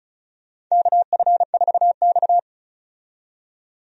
Morse code, heard as a keyed tone, about 34 wpm, 700 Hz. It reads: KF4X
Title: USA Call Signs - 4 Characters - 35wpm